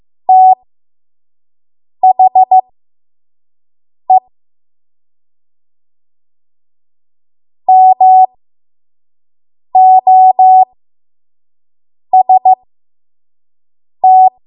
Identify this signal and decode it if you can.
Morse code, heard as a keyed tone, about 15 wpm, spaced as for 5 wpm, 750 Hz. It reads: THE MOST
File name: Morse / 05WPM